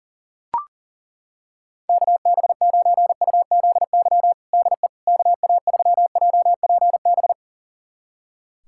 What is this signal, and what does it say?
Morse code, heard as a keyed tone, about 37 wpm, 700 Hz. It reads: KB9UZY DE KA3JPB
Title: USA Callsign DE Callsign - 6 Characters - 40wpm